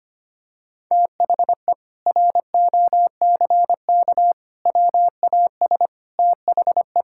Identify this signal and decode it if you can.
Morse code, heard as a keyed tone, about 25 wpm, 700 Hz. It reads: THE ROCK WAS THE
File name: practice / 25wpm / Sentences with 4-Word Phrases - 25wpm